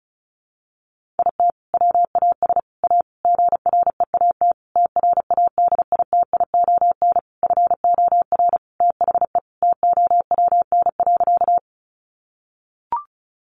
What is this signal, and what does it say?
Morse code, heard as a keyed tone, about 34 wpm, 700 Hz. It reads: IT WAS A GREAT TRADITION FOR THE TOWN.
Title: Sentences from Top 1500 Words 35wpm